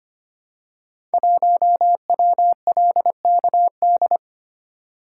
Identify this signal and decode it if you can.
Morse code, heard as a keyed tone, about 25 wpm, 700 Hz. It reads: 1WLKD
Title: Code Groups - 5 Letter:Numbers - 25wpm